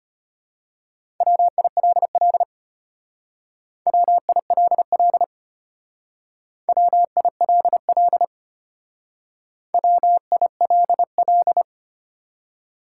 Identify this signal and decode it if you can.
Morse code, heard as a keyed tone, about 32 wpm, 700 Hz. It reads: WILL WILL WILL WILL